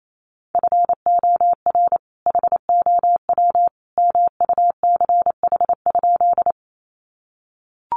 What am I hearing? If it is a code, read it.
Morse code, heard as a keyed tone, about 28 wpm, 700 Hz. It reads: FOR HOW MUCH?
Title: Sentences from Top 200 Words 28wpm